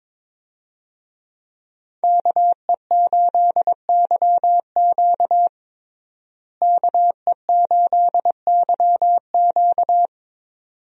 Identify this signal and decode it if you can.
Morse code, heard as a keyed tone, about 22 wpm, 700 Hz. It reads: KE8YQ KE8YQ
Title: USA Call Signs - 5 Characters - Repeated 2x - 22wpm